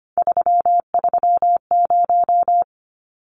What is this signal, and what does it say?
Morse code, heard as a keyed tone, about 25 wpm, 700 Hz. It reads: 330
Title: Numbers - 3-Digit 25wpm